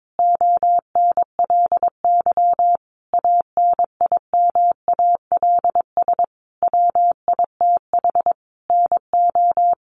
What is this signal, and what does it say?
Morse code, heard as a keyed tone, about 22 wpm, 700 Hz. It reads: ONLY ANIMALS WITH NO
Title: A Sound Of Thunder 22wpm